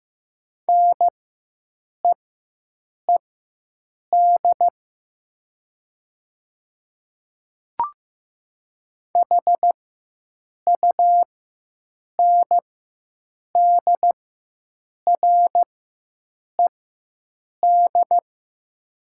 Morse code, heard as a keyed tone, about 15 wpm, spaced as for 7 wpm, 700 Hz. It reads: NEED HUNDRED
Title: 4x Character Spacing - Top 500 Words - 15wpm